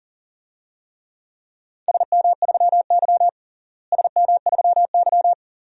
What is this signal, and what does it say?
Morse code, heard as a keyed tone, about 40 wpm, 700 Hz. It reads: SM3Y SM3Y
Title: DXCC-100 Call Signs - 4 Characters - Repeated 2x - 40wpm